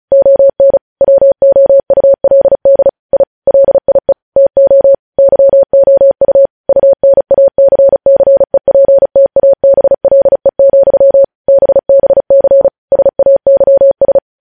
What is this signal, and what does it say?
Morse code, heard as a keyed tone, about 36 wpm, 550 Hz. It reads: ON WOULD I LIE TO YOU UNACCEPTABLE, BBC SAYS
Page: news headlines in morse code at 35 wpm